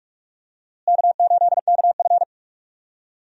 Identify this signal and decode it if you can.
Morse code, heard as a keyed tone, about 44 wpm, 700 Hz. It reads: K8KF
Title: USA Call Signs - 4 Characters - Repeated 2x - 45wpm